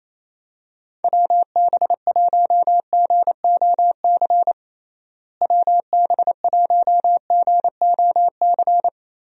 Morse code, heard as a keyed tone, about 27 wpm, 700 Hz. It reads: WB1GOC WB1GOC
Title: USA Call Signs - 6 Characters - Repeated 3x - 28wpm